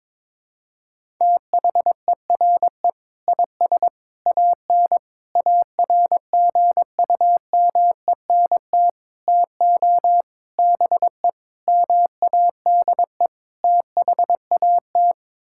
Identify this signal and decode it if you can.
Morse code, heard as a keyed tone, about 22 wpm, 700 Hz. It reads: THERE IS AN ARGUMENT TO BE MADE THAT
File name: practice / 22wpm / Sets of 8 Words - 22wpm